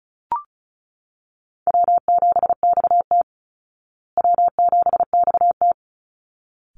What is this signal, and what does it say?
Morse code, heard as a keyed tone, about 34 wpm, 700 Hz. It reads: W7XT W7XT
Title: Easy Call Signs Repeated 2x - 35wpm